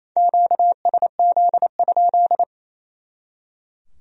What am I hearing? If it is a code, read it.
Morse code, heard as a keyed tone, about 36 wpm, 700 Hz. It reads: QSZ?